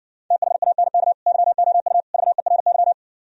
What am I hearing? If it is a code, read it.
Morse code, heard as a keyed tone, about 59 wpm, 700 Hz. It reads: THANK YOU VERY